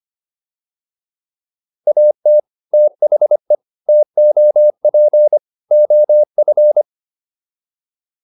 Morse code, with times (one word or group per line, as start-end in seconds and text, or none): 1.9-2.4 AT
2.7-3.5 THE
3.9-5.4 TOP
5.7-6.8 OF